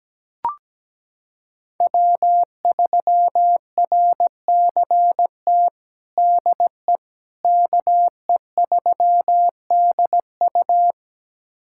Morse code, heard as a keyed tone, about 17 wpm, 700 Hz. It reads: W3RCT DE KE3DU
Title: USA Callsign DE Callsign - 5 Characters - 17wpm